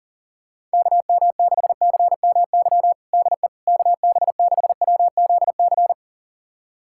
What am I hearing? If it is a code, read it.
Morse code, heard as a keyed tone, about 37 wpm, 700 Hz. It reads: KM6CMY DE KB6WZC
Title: USA Callsign DE Callsign - 6 Characters - 40wpm